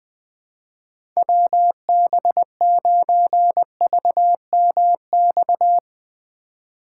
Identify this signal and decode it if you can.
Morse code, heard as a keyed tone, about 20 wpm, 700 Hz. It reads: WB9VMX
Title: Speed Racing - USA 2x3 Call Signs 17wpm